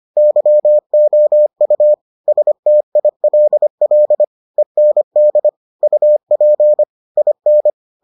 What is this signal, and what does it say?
Morse code, heard as a keyed tone, about 25 wpm, 600 Hz. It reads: YOU STILL END UP IN